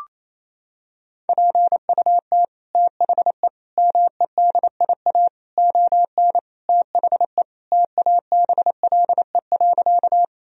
Morse code, heard as a keyed tone, about 27 wpm, 700 Hz. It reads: PUT THE MEDIA ON THE TABLE.
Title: Sentences from Top 1000 Words - Review Part 4 - 28wpm